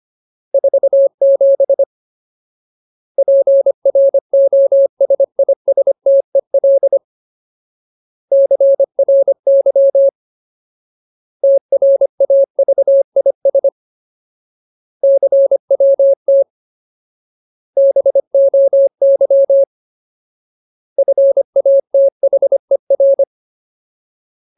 Morse code, heard as a keyed tone, about 25 wpm, 550 Hz. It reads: 47 PROSISTEL CRY TRAVIS CWT BOY FATHER
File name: Ultimate Word List / 25WPM Ultimate